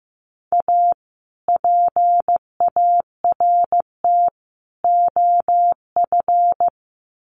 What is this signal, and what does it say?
Morse code, heard as a keyed tone, about 15 wpm, 700 Hz. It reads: A PART OF
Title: Sets of 3 Words 15wpm